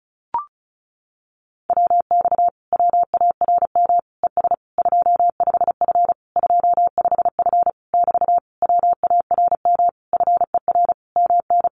Morse code, heard as a keyed tone, about 34 wpm, 700 Hz. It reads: WX WARM ES 25F 25F <BT> WARM FER MN
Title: Common QSO Phrases Encore 35wpm